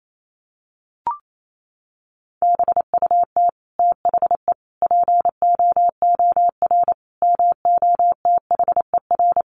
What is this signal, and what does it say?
Morse code, heard as a keyed tone, about 28 wpm, 700 Hz. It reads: BUT THE POOR MOTHER
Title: Sentences from Top 1000 Words 28wpm